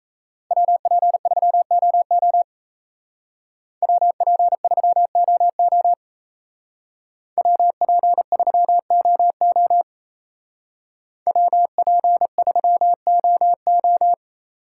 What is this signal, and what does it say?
Morse code, heard as a keyed tone, about 35 wpm, 700 Hz. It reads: WP3OO WP3OO WP3OO WP3OO
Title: Speed Racing - USA 2x2 Call Signs 28wpm